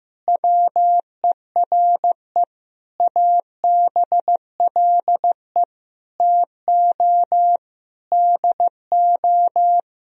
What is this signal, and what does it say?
Morse code, heard as a keyed tone, about 15 wpm, 700 Hz. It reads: WERE ABLE TO DO
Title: Sentences from Top 1000 Words - Review Part 2 - 15wpm